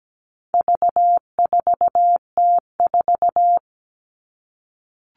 Morse code, heard as a keyed tone, about 17 wpm, 700 Hz. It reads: V4T4
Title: All Cut Numbers - 4-Digit 17wpm